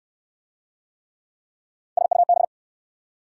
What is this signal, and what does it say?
Morse code, heard as a keyed tone, about 69 wpm, 690 Hz. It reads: SUB